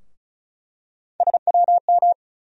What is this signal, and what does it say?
Morse code, heard as a keyed tone, about 36 wpm, 700 Hz. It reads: SWM
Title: USA Call Sign Postfixes - 3 Letters 35wpm